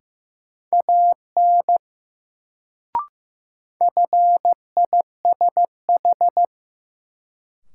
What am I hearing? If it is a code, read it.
Morse code, heard as a keyed tone, about 15 wpm, 700 Hz. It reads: AN FISH
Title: Sentences from Top 300 Words - New Word Drill - 15wpm